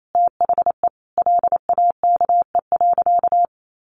Morse code, heard as a keyed tone, about 28 wpm, 700 Hz. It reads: THE LAKE.
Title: Sentences from Top 1500 Words 28wpm